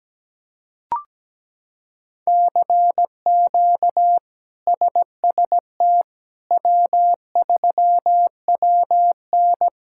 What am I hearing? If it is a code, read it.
Morse code, heard as a keyed tone, about 17 wpm, 700 Hz. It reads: CQ SST W3WN
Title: K1USN SST QSO Practice - 17wpm